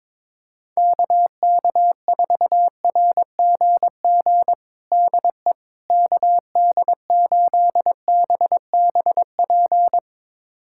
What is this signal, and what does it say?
Morse code, heard as a keyed tone, about 22 wpm, 700 Hz. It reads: KK4RGG DE KD8BBP
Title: USA Callsign DE Callsign 22wpm